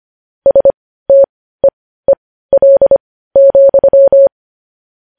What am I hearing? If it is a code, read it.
Morse code, heard as a keyed tone, about 25 wpm, spaced as for 15 wpm, 550 Hz. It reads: STEEL,
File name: podcast / News headlines-1x-25x15